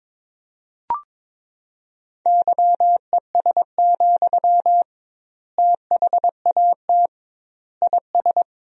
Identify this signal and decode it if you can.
Morse code, heard as a keyed tone, about 22 wpm, 700 Hz. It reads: YES, THAT IS
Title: 2x Word-Spacing - Sentences from Top 500 Words - 22wpm